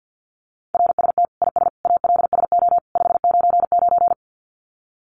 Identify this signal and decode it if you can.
Morse code, heard as a keyed tone, about 50 wpm, 700 Hz. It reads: RST IS ALSO 599
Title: Common QSO Phrases 50wpm